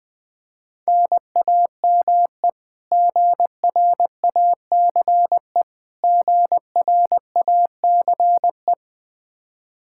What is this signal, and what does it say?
Morse code, heard as a keyed tone, about 20 wpm, 700 Hz. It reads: NAME GRACE GRACE
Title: Practice QSO - Exchange 1 - Beginning - 20wpm